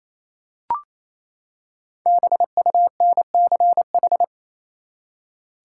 Morse code, heard as a keyed tone, about 28 wpm, 700 Hz. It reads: BUNCH
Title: Five Letter Words - 28wpm